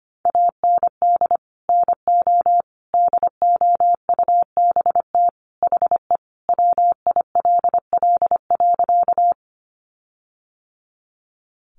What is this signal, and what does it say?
Morse code, heard as a keyed tone, about 25 wpm, 700 Hz. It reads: AND NO DOUBT HE WILL.
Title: Sentences from Top 1500 Words 25wpm